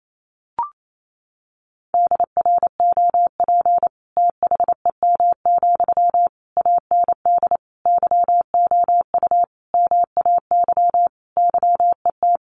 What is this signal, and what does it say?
Morse code, heard as a keyed tone, about 28 wpm, 700 Hz. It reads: DROP THEM, AND YOU MAY YET
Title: Sentences from Top 1400 Words 28wpm